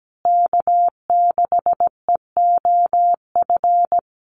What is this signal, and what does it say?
Morse code, heard as a keyed tone, about 17 wpm, 700 Hz. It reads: K6EOF
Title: USA Call Signs Repeated 3x 17wpm